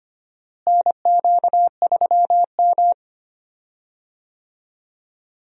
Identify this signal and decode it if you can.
Morse code, heard as a keyed tone, about 25 wpm, 700 Hz. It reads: NQ3M
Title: USA Call Signs - 4 Characters - Repeated 2x - 25wpm